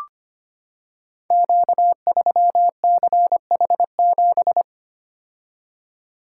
Morse code, heard as a keyed tone, about 25 wpm, 700 Hz. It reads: Q3CH7
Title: Code Groups - 5 Letter:Numbers - 25wpm